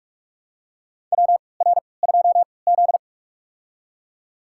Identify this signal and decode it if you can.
Morse code, heard as a keyed tone, about 44 wpm, spaced as for 25 wpm, 690 Hz. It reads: WR2Z
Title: USA Call Signs - 4 Characters - 3x Character Spacing - 45wpm